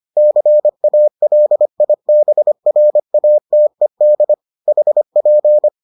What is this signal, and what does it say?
Morse code, heard as a keyed tone, about 25 wpm, 600 Hz. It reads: CALIBRATED HP